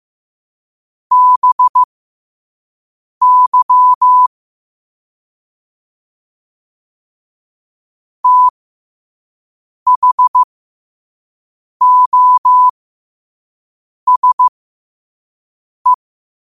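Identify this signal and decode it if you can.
Morse code, heard as a keyed tone, about 15 wpm, spaced as for 5 wpm, 1000 Hz. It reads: BY THOSE
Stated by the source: Morse code sent at 5 wpm.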